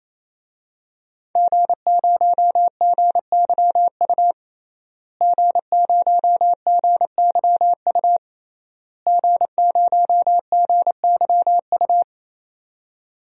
Morse code, heard as a keyed tone, about 28 wpm, 700 Hz. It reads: G0GYU G0GYU G0GYU
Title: DXCC-100 Call Signs - 5 Characters - Repeated 3x - 28wpm